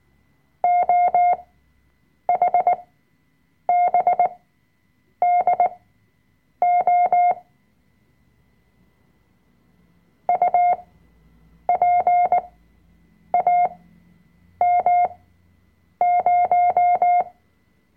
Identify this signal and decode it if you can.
Morse code, heard as a keyed tone, about 19 wpm, spaced as for 7 wpm, 710 Hz. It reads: OHBDO UPAM0
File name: Workshop Dateien / CW-Online-Kurs / Lektion 005